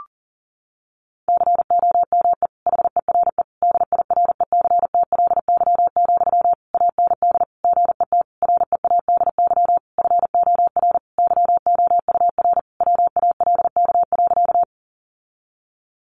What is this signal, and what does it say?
Morse code, heard as a keyed tone, about 40 wpm, 700 Hz. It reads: COME HERE DIRECTLY, AND GET READY FOR YOUR WALK.
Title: Sentences from Top 800 Words 40wpm